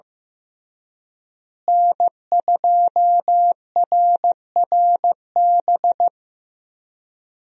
Morse code, heard as a keyed tone, about 15 wpm, 700 Hz. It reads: N2RRB
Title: USA Call Signs - 5 Characters - Repeated 3x - 15wpm